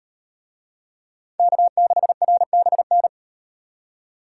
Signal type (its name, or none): Morse code